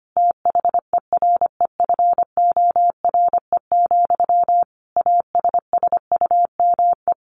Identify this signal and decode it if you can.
Morse code, heard as a keyed tone, about 25 wpm, 700 Hz. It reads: THEREFORE, ASSUME
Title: Sentences from Top 1500 Words 25wpm